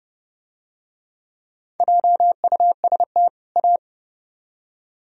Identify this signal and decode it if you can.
Morse code, heard as a keyed tone, about 29 wpm, 700 Hz. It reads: JUST A